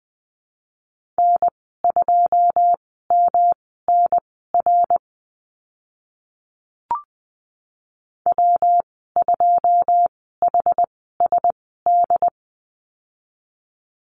Morse code, heard as a keyed tone, about 19 wpm, spaced as for 14 wpm, 700 Hz. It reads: N2MNR W2HSD
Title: 140 Lesson 23g - Letter f - Call Signs 20wpm